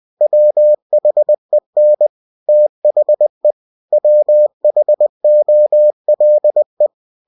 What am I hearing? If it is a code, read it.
Morse code, heard as a keyed tone, about 20 wpm, 600 Hz. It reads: WHEN THE WHOLE
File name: cw-books / winnie-the-pooh / winnie-20wpm-page-0094